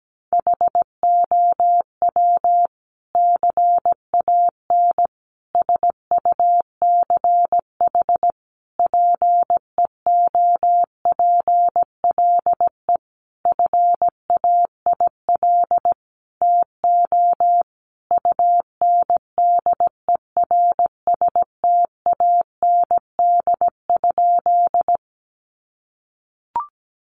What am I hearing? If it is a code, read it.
Morse code, heard as a keyed tone, about 17 wpm, 700 Hz. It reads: HOW CAN SUCH PEOPLE FAIL TO UNDERSTAND?
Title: Sentences from Top 1100 Words 17wpm